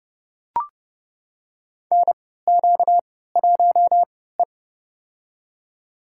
Morse code, heard as a keyed tone, about 31 wpm, spaced as for 17 wpm, 700 Hz. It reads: NQ1E
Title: USA Call Signs - 4 Characters - 3x Character Spacing - 30wpm